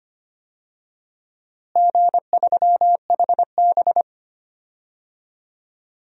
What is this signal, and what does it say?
Morse code, heard as a keyed tone, about 25 wpm, 700 Hz. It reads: G3HB
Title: DXCC-100 Call Signs - 4 Characters - Repeated 2x - 25wpm